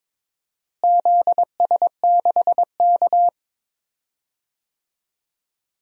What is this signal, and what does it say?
Morse code, heard as a keyed tone, about 22 wpm, 700 Hz. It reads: ZS6K
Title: DXCC-100 Call Signs - 4 Characters - Repeated 2x - 22wpm